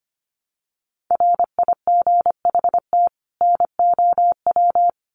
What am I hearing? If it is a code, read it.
Morse code, heard as a keyed tone, about 25 wpm, 700 Hz. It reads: RIGHT NOW
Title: Sets of 2 Words 25wpm